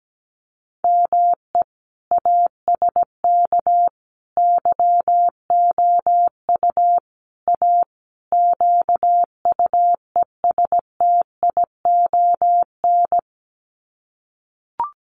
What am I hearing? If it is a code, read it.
Morse code, heard as a keyed tone, about 17 wpm, 700 Hz. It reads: ME ASK YOU A QUESTION
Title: Sets of 5 Words Encore2 17wpm